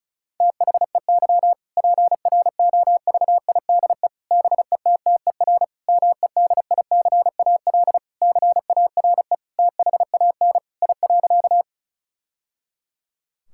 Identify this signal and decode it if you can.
Morse code, heard as a keyed tone, about 36 wpm, 700 Hz. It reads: THEY PROVIDE BETTER MEDICAL CARE THAN I.
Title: Sentences from Top 1000 Words - Review Part 5 - 35wpm